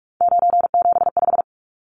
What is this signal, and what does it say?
Morse code, heard as a keyed tone, about 44 wpm, 700 Hz. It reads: 975
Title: Numbers - 3-Digit 45wpm